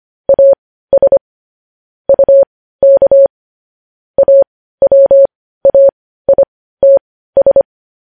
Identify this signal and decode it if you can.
Morse code, heard as a keyed tone, about 25 wpm, spaced as for 15 wpm, 550 Hz. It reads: AS UK AWAITS
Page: news headlines in morse code at 15 wpm